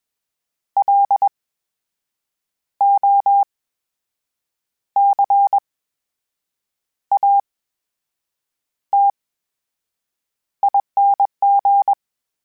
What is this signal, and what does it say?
Morse code, heard as a keyed tone, about 21 wpm, spaced as for 7 wpm, 800 Hz. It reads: LOCATING
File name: Special INGs-mp3s / Special INGs-mp3s / VerySlowToFaster Ing @20